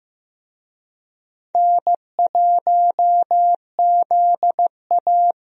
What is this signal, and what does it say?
Morse code, heard as a keyed tone, about 16 wpm, 700 Hz. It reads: N1ZA
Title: USA Call Signs - 4 Characters - Repeated 2x - 15wpm